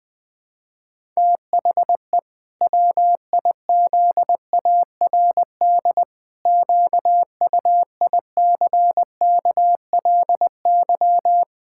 Morse code, heard as a keyed tone, about 20 wpm, 700 Hz. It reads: THE WIZARD QUICKLY
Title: Pangram Challenge v2 - 20wpm